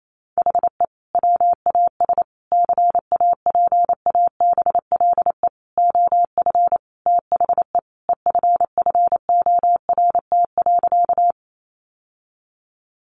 Morse code, heard as a keyed tone, about 28 wpm, 700 Hz. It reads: HE WAS CAPABLE OF THE EFFORT.
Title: Sentences from Top 1200 Words 28wpm